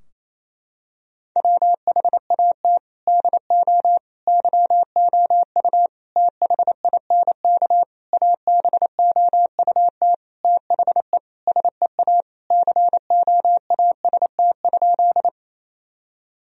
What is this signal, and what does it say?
Morse code, heard as a keyed tone, about 27 wpm, 700 Hz. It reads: WHAT DO YOU THINK ABOUT THE SEA COAST?
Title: Sentences from Top 1000 Words - Review Part 4 - 28wpm